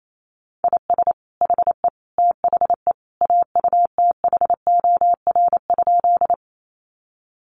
Morse code, heard as a keyed tone, about 28 wpm, 700 Hz. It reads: IS HE THE AUTHOR?
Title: Sentences from Top 1200 Words 28wpm